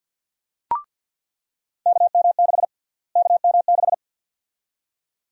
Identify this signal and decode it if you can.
Morse code, heard as a keyed tone, about 50 wpm, 700 Hz. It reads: KM6 KM6
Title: USA Call Sign Prefixes - Repeated 2x - 50wpm